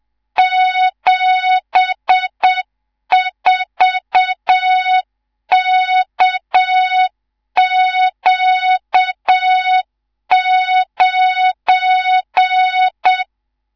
Morse code, heard as a keyed tone, about 7 wpm, 3700 Hz, with noise. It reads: 74KQ9
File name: Morze-HA2MN / hang / 19 GYAKORLAT